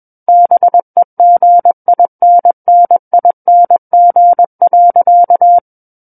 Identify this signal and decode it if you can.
Morse code, heard as a keyed tone, about 21 wpm, 700 Hz. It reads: BEGINNING.
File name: Book One / 12-Chapter 12 - 21 WPM